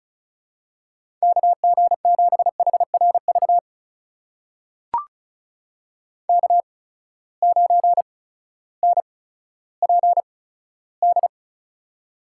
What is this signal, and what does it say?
Morse code, heard as a keyed tone, about 36 wpm, spaced as for 16 wpm, 700 Hz. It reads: KG7HRV K9NPD
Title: USA Call Signs - 8x Character Spacing - 35wpm